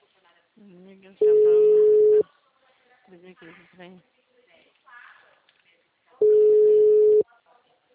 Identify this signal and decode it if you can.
Ringback tone (Europe, 425 Hz), with speech.